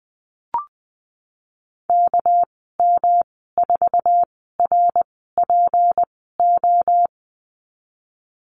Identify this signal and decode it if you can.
Morse code, heard as a keyed tone, about 20 wpm, spaced as for 15 wpm, 700 Hz. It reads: KM4RPO
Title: 207 Lesson 33g - Letter j - Call Signs 20wpm